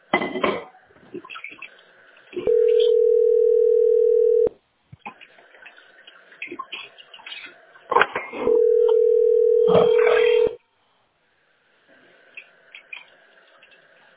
Ringback tone (North America), 440 + 480 Hz.